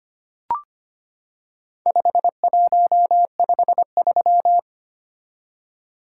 Morse code, heard as a keyed tone, about 25 wpm, 700 Hz. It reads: 5153
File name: practice / 25wpm / Four Letter-Number Combinations - 25wpm